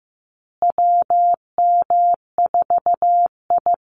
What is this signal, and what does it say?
Morse code, heard as a keyed tone, about 15 wpm, 700 Hz. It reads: WM4I
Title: Call Signs Repeated 3x - 15wpm